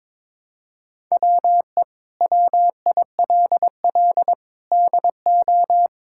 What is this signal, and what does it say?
Morse code, heard as a keyed tone, about 22 wpm, 700 Hz. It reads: WE WILL DO
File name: practice / 22wpm / Sets of 7 Words - 22wpm